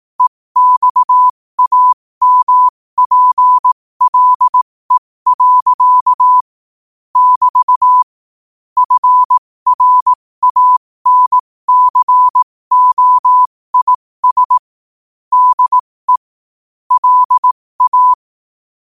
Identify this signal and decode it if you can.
Morse code, heard as a keyed tone, about 18 wpm, spaced as for 15 wpm, 1000 Hz. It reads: EXAMPLE. <BT> FRANCOIS DE LA
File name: qotdfiles / 15wpm